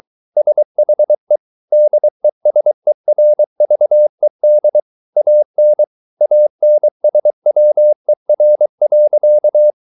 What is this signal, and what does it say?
Morse code, heard as a keyed tone, about 23 wpm, 600 Hz. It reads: SHE DESERVED AN ANSWER.